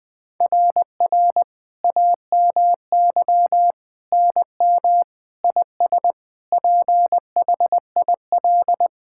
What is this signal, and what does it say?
Morse code, heard as a keyed tone, about 21 wpm, 700 Hz. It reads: RR AMY NM IS PHIL
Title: SOTA QSO Practice 20wpm